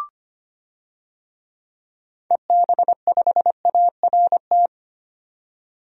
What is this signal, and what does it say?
Morse code, heard as a keyed tone, about 25 wpm, 700 Hz. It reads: EB5ART
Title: DXCC-100 Call Signs - 6 Characters - 25wpm